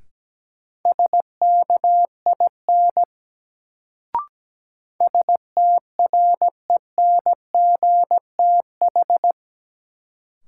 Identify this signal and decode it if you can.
Morse code, heard as a keyed tone, about 17 wpm, 700 Hz. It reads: SKIN STRENGTH
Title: Sentences from Top 1300 Words - New Word Drill Alt - 17wpm